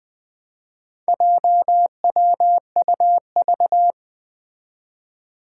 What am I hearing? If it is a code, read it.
Morse code, heard as a keyed tone, about 20 wpm, 700 Hz. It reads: JWUV
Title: Four Challenging Letter-Number Combinations 20wpm